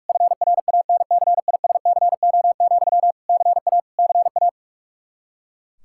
Morse code, heard as a keyed tone, about 43 wpm, 700 Hz. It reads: FRANCISCO, CA CA
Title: Practice QSO - Exchange 1 - Beginning - 45wpm